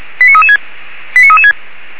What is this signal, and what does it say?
Five-tone selective call (ZVEI), heard as quick runs of five tones, tones of about 80 ms. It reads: 89336 89336
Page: Index of /464/SDR/ZVEI/89315/2021.10.30